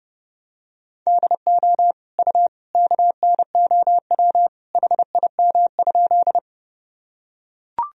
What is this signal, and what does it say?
Morse code, heard as a keyed tone, about 30 wpm, 700 Hz. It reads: DO U KNOW HIM?
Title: Sentences from Top 100 Words and Abbreviations 30wpm